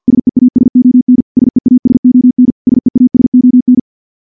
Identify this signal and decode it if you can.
Morse code, heard as a keyed tone, about 49 wpm, 250 Hz. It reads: SEASON SEASON SEASON
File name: morse / morse-training-50WPM